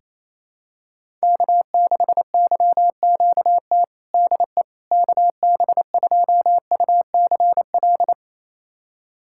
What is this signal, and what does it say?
Morse code, heard as a keyed tone, about 28 wpm, 700 Hz. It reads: K6YQT DE KB2UCL
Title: USA Callsign DE Callsign 28wpm